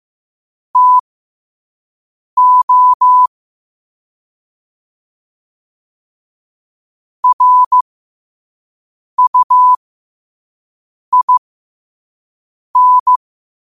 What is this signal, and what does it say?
Morse code, heard as a keyed tone, about 15 wpm, spaced as for 5 wpm, 1000 Hz. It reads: TO RUIN